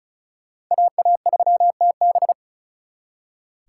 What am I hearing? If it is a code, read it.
Morse code, heard as a keyed tone, about 45 wpm, 700 Hz. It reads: AA3TB